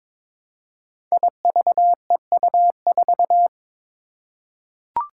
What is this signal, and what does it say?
Morse code, heard as a keyed tone, about 22 wpm, 700 Hz. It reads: IVEU4
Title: Five Challenging Letter-Number Combinations 22wpm